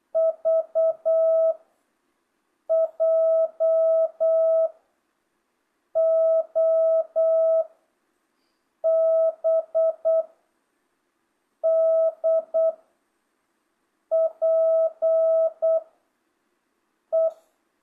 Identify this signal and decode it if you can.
Morse code, heard as a keyed tone, about 8 wpm, 660 Hz. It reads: V J O B D P E
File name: jh8cbh / 2025cw